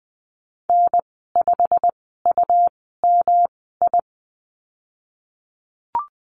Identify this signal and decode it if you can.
Morse code, heard as a keyed tone, about 20 wpm, spaced as for 15 wpm, 700 Hz. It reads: N5UMI
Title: 154 Lesson 25g - Letter p - Call Signs 20wpm